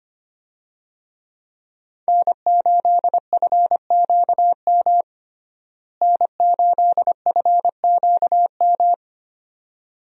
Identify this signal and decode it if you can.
Morse code, heard as a keyed tone, about 25 wpm, 700 Hz. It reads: N8FQM N8FQM
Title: USA Call Signs - 5 Characters - Repeated 2x - 25wpm